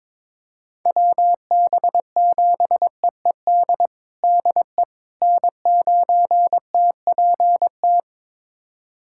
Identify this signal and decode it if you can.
Morse code, heard as a keyed tone, about 22 wpm, 700 Hz. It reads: WB7EED DE N9TPT
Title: USA Callsign DE Callsign 22wpm